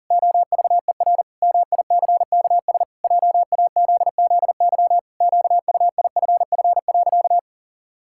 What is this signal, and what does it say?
Morse code, heard as a keyed tone, about 40 wpm, 700 Hz. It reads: OVER MICKS JAZZY QUIFF.
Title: Pangram Challenge v2 - 40wpm